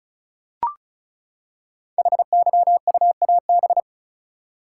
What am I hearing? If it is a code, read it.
Morse code, heard as a keyed tone, about 36 wpm, 700 Hz. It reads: HYUAB
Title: Code Groups - 5 Letter:Numbers - 35wpm